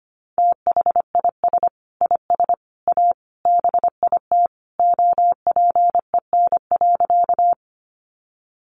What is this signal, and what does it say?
Morse code, heard as a keyed tone, about 25 wpm, 700 Hz. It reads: THIS IS A BIT OPEN.
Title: Sentences from Top 900 Words 25wpm